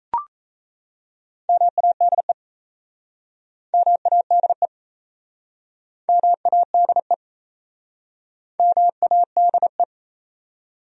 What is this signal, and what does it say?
Morse code, heard as a keyed tone, about 38 wpm, 700 Hz. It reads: MADE MADE MADE MADE